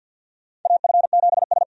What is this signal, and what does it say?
Morse code, heard as a keyed tone, about 52 wpm, 700 Hz. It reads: AF7S
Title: USA Call Signs - 4 Characters - 50wpm